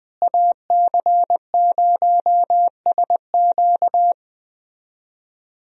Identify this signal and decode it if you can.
Morse code, heard as a keyed tone, about 20 wpm, 700 Hz. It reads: AC0SQ
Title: USA Call Signs - 5 Characters - Repeated 3x - 20wpm